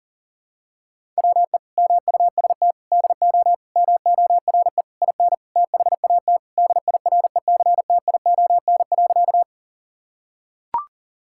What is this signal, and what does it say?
Morse code, heard as a keyed tone, about 40 wpm, 700 Hz. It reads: WE MUST DO MORE IN THAT DIRECTION.
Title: Sentences from Top 1000 Words - Review Part 5 - 40wpm